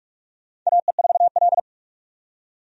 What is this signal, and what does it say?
Morse code, heard as a keyed tone, about 46 wpm, 700 Hz. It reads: AE4L
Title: USA Call Signs - 4 Characters - 45wpm